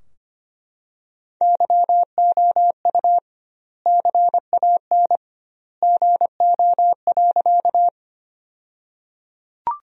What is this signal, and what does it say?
Morse code, heard as a keyed tone, about 25 wpm, 700 Hz. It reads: YOU CAN GO.
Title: 2x Word-Spacing - Sentences from Top 100 Words - 25wpm